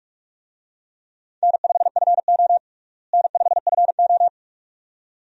Morse code, heard as a keyed tone, about 47 wpm, 700 Hz. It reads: N5FO N5FO
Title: USA Call Signs - 4 Characters - Repeated 2x - 45wpm